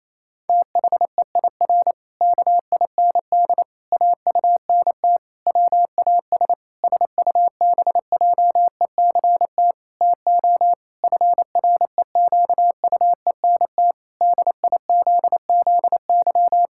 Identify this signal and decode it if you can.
Morse code, heard as a keyed tone, about 28 wpm, 700 Hz. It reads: THEIR KIND AUNT WAS SUBJECT TO FREQUENT DIZZY
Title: Pangram Challenge v2 - 28wpm